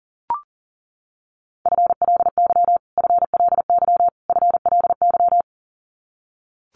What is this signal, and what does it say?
Morse code, heard as a keyed tone, about 40 wpm, 700 Hz. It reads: FLY FLY FLY
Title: Top 500 Words Repeated 40wpm